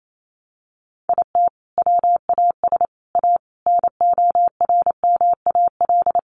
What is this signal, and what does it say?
Morse code, heard as a keyed tone, about 28 wpm, 700 Hz. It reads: IT WAS A NORMAL
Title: Sentences from Top 1200 Words 28wpm